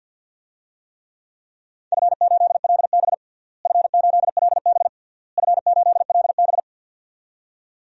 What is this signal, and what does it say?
Morse code, heard as a keyed tone, about 46 wpm, 700 Hz. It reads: F8LB F8LB F8LB
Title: DXCC-100 Call Signs - 4 Characters - Repeated 3x - 50wpm